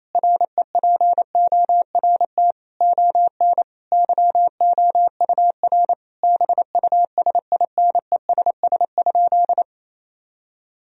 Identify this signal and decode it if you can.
Morse code, heard as a keyed tone, about 28 wpm, 700 Hz. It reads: REPORT ON YOUR BUSINESS?
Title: Sentences from Top 1000 Words - Review Part 4 - 28wpm